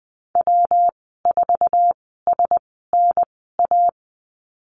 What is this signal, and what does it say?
Morse code, heard as a keyed tone, about 20 wpm, spaced as for 14 wpm, 700 Hz. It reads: W4SNA
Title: 154 Lesson 25g - Letter p - Call Signs 20wpm